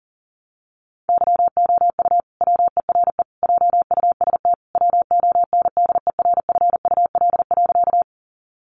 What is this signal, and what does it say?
Morse code, heard as a keyed tone, about 40 wpm, 700 Hz. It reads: YOU WERE JUST WONDERFUL.
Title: Sentences from Top 1100 Words 40wpm